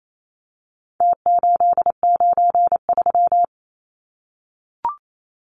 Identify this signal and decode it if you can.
Morse code, heard as a keyed tone, about 28 wpm, 700 Hz. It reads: T893
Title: Numbers - Contest Style - 4-Digit 28wpm